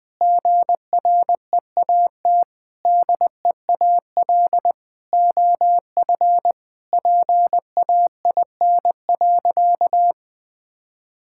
Morse code, heard as a keyed tone, about 21 wpm, 700 Hz. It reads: GREAT DEAL OF PAIN.
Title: Sentences from Top 1600 Words 20wpm